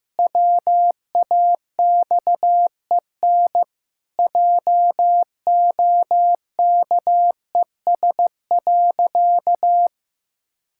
Morse code, heard as a keyed tone, about 15 wpm, 700 Hz. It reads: WAXEN JOKES.
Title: Pangram Challenge v2 - 15wpm